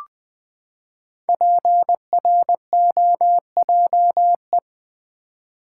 Morse code, heard as a keyed tone, about 20 wpm, 700 Hz. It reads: PROJE
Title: Build A Word - 5 Letters At A Time - 10-Letter Words - 20wpm